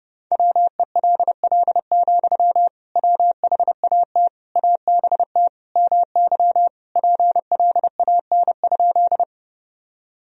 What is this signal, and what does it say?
Morse code, heard as a keyed tone, about 29 wpm, 700 Hz. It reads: WELL, WHAT ABT MY PLAN?
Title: Sentences from Top 500 Words and Abbreviations 30wpm